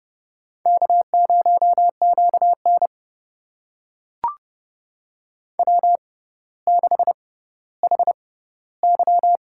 Morse code, heard as a keyed tone, about 28 wpm, spaced as for 15 wpm, 700 Hz. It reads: K0QN W6HY
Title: USA Call Signs - 4 Characters - 6x Character Spacing - 30wpm